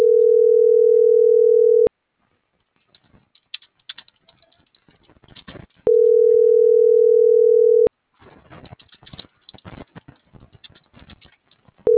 Ringback tone (North America), 440 + 480 Hz.